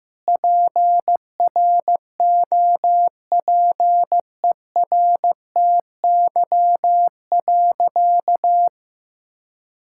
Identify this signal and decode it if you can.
Morse code, heard as a keyed tone, about 15 wpm, 700 Hz. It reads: PROPERTY.
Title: Sentences from Top 1600 Words 15wpm